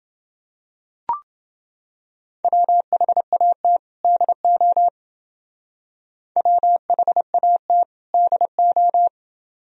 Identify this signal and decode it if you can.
Morse code, heard as a keyed tone, about 28 wpm, 700 Hz. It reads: WHAT DO WHAT DO